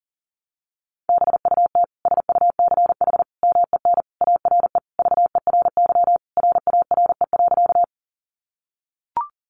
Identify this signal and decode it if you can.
Morse code, heard as a keyed tone, about 40 wpm, 700 Hz. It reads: BUT SUCH MEN ARE VERY RARE.
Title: Sentences from Top 1100 Words 40wpm